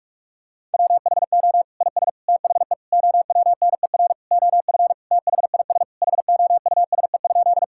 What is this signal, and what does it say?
Morse code, heard as a keyed tone, about 46 wpm, 700 Hz. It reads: WHO IS THE OWNER OF THIS HOUSE?
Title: Sentences from Top 1600 Words 45wpm